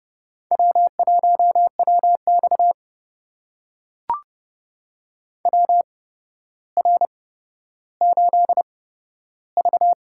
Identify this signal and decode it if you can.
Morse code, heard as a keyed tone, about 28 wpm, spaced as for 13 wpm, 700 Hz. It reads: W1WX WR8V
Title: USA Call Signs - 4 Characters - 8x Character Spacing - 30wpm